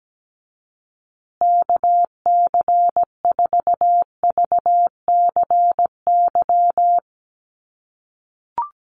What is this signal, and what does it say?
Morse code, heard as a keyed tone, about 17 wpm, 700 Hz. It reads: KC4VCY
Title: USA Call Signs Repeated 2x 17wpm